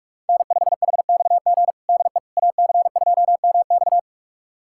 Morse code, heard as a keyed tone, about 42 wpm, 700 Hz. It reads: N5HXG DE AC2MX
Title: USA Callsign DE Callsign - 5 Characters - 45wpm